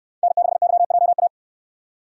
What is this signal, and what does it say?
Morse code, heard as a keyed tone, about 68 wpm, 700 Hz. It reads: N6YJA